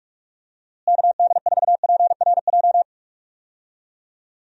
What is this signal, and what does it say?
Morse code, heard as a keyed tone, about 43 wpm, 700 Hz. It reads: KD4PRJ
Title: USA Call Signs - 6 Characters - Repeated 2x - 45wpm